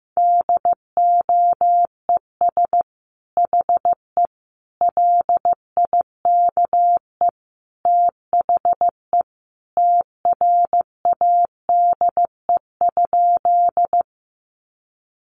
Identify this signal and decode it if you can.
Morse code, heard as a keyed tone, about 15 wpm, 700 Hz. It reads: DOES HE LIKE THE TRADE?
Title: Sentences from Top 600 Words 15wpm